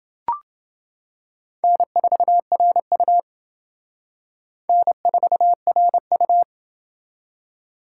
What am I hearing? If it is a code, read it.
Morse code, heard as a keyed tone, about 28 wpm, 700 Hz. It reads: N4RU N4RU